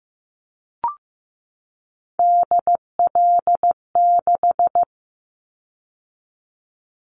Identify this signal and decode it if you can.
Morse code, heard as a keyed tone, about 15 wpm, 700 Hz. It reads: DL6
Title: Callsign Prefixes for Top 25 DXCC - 15wpm